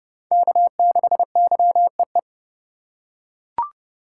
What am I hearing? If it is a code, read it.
Morse code, heard as a keyed tone, about 30 wpm, 700 Hz. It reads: K6YEE
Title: USA Call Signs - 2x Character Spacing - 30wpm